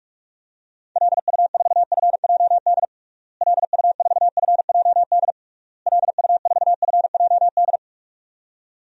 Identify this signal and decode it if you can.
Morse code, heard as a keyed tone, about 46 wpm, 700 Hz. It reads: LU4FJD LU4FJD LU4FJD
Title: DXCC-100 Call Signs - 6 Characters - Repeated 3x - 45wpm